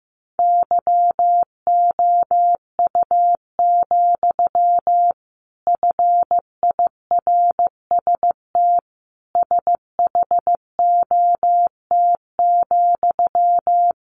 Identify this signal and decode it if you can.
Morse code, heard as a keyed tone, about 15 wpm, 700 Hz. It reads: YOU, FIRST SHOT,
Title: A Sound Of Thunder 15wpm